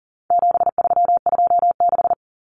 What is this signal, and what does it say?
Morse code, heard as a keyed tone, about 40 wpm, 700 Hz. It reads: 7326
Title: Common Cut Numbers - 4-Digit 40wpm